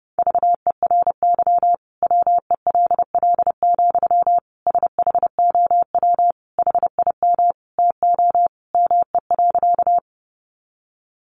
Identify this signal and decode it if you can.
Morse code, heard as a keyed tone, about 30 wpm, 700 Hz. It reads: VERY WELL, SHOW HIM TO ME.
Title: Sentences from Top 200 Words 30wpm